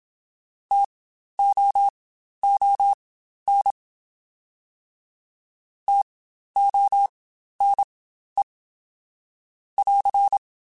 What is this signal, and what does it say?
Morse code, heard as a keyed tone, about 27 wpm, spaced as for 12 wpm, 800 Hz. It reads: TOON TONE <AR>
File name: Spaced Alphas / Spaced Alphas mp3s / Ts sp @25